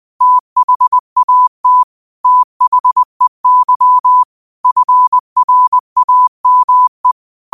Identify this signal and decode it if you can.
Morse code, heard as a keyed tone, about 20 wpm, 1000 Hz. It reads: THAT THEY FRAME
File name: qotdfiles / 20wpm